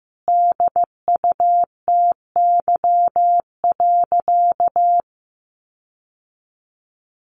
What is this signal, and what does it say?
Morse code, heard as a keyed tone, about 15 wpm, 700 Hz. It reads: DUTY.
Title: Sentences from Top 1500 Words 15wpm